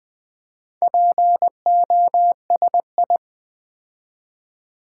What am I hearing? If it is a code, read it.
Morse code, heard as a keyed tone, about 20 wpm, 700 Hz. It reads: POSI
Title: Build A Word - 4 Letters At A Time - 8-Letter Words - 20wpm